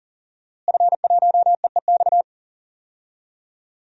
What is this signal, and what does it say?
Morse code, heard as a keyed tone, about 40 wpm, 700 Hz. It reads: F1EEX
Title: DXCC-100 Call Signs - 5 Characters - Repeated 2x - 40wpm